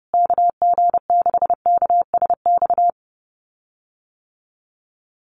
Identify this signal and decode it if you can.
Morse code, heard as a keyed tone, about 30 wpm, 700 Hz. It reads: KG6KSX
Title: USA Call Signs Repeated 3x 30wpm